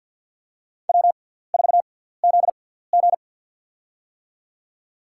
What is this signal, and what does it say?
Morse code, heard as a keyed tone, about 50 wpm, spaced as for 17 wpm, 700 Hz. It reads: W4ZG
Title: USA Call Signs - 4 Characters - 6x Character Spacing - 50wpm